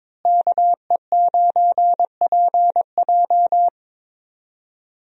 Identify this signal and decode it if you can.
Morse code, heard as a keyed tone, about 21 wpm, 700 Hz. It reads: KE9PJ
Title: USA Call Signs - 5 Characters - Repeated 2x - 22wpm